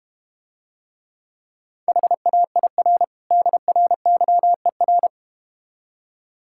Morse code, heard as a keyed tone, about 32 wpm, 700 Hz. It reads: HAIR DRYER